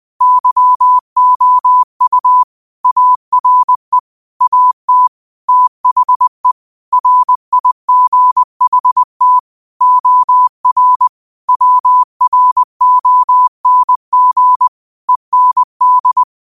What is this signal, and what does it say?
Morse code, heard as a keyed tone, about 20 wpm, 1000 Hz. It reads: YOU ARE AT THE RIGHT OR WRONG END
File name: qotdfiles / 20wpm